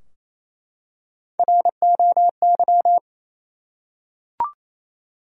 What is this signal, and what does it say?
Morse code, heard as a keyed tone, about 27 wpm, 700 Hz. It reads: ROY
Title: USA Common Names - Max 3 Letters - 28wpm